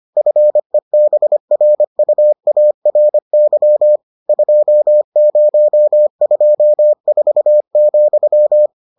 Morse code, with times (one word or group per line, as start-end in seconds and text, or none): 0.2-4.0 FEBRUARY
4.3-8.7 2024,